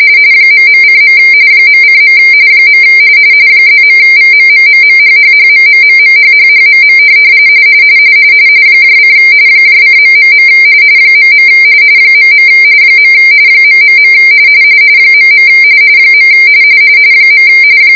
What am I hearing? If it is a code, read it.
RTTY, about 45.45 baud. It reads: ..:. .:. +.VMMMMHHHMMMMHHHMM : :... .. .. ... .. .